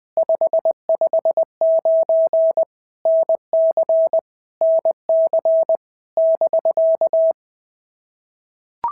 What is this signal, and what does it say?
Morse code, heard as a keyed tone, about 20 wpm, 650 Hz. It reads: 559 NC NC <BK>
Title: POTA QSO Practice - 20wpm